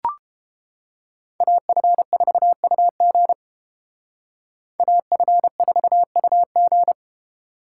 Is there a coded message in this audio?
Yes: Morse code from 1.4 to 6.9 s, heard as a keyed tone, about 29 wpm, 700 Hz. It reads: AF4UG AF4UG